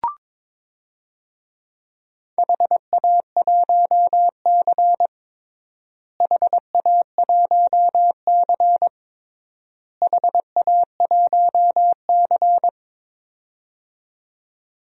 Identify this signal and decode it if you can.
Morse code, heard as a keyed tone, about 21 wpm, 700 Hz. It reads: HA1C HA1C HA1C
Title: DXCC-100 Call Signs - 4 Characters - Repeated 3x - 22wpm